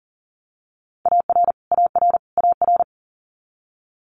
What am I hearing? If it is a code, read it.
Morse code, heard as a keyed tone, about 40 wpm, 700 Hz. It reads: AR AR AR
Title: Multi-letter Phonemes Repeated 3x 40wpm